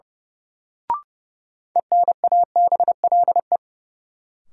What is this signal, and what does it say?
Morse code, heard as a keyed tone, about 30 wpm, 700 Hz. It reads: ENABLE
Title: Sentences from Top 1500 Words - New Word Drill Alt - 30wpm